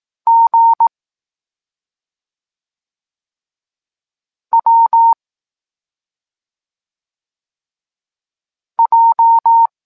Morse code, heard as a keyed tone, about 18 wpm, 920 Hz. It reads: G W J
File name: Tekst-23 4